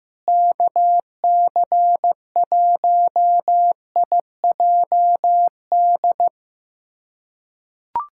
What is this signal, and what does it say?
Morse code, heard as a keyed tone, about 15 wpm, 700 Hz. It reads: KC1IJD
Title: USA Callsign DE Callsign - 6 Characters - 15wpm